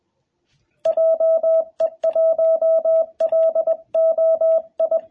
Morse code, heard as a keyed tone, about 20 wpm, 660 Hz. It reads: JE1LOI
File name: jh8cbh / 2021cw